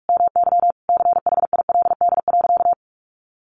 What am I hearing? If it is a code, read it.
Morse code, heard as a keyed tone, about 44 wpm, 700 Hz. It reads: MY CHILD.
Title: Sentences from Top 1300 Words 45wpm